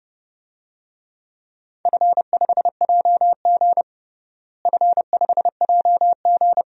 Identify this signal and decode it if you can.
Morse code, heard as a keyed tone, about 30 wpm, 700 Hz. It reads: F5JG F5JG
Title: DXCC-100 Call Signs - 4 Characters - Repeated 2x - 30wpm